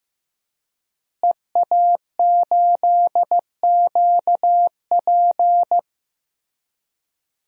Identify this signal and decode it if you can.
Morse code, heard as a keyed tone, about 15 wpm, 700 Hz. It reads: EA8QP
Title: DXCC-100 Call Signs - 5 Characters - Repeated 3x - 15wpm